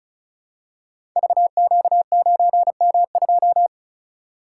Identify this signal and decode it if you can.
Morse code, heard as a keyed tone, about 36 wpm, 700 Hz. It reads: VQ9M2
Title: Code Groups - 5 Letter:Numbers - 35wpm